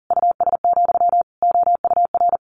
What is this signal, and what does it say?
Morse code, heard as a keyed tone, about 40 wpm, 700 Hz. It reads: US, OUR
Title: A Sound Of Thunder 40wpm